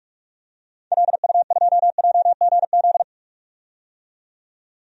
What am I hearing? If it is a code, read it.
Morse code, heard as a keyed tone, about 42 wpm, 700 Hz. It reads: LU2JGZ
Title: DXCC-100 Call Signs - 6 Characters - 45wpm